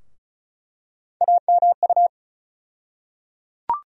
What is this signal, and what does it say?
Morse code, heard as a keyed tone, about 36 wpm, 700 Hz. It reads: AMU
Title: USA Call Sign Postfixes - 3 Letters 35wpm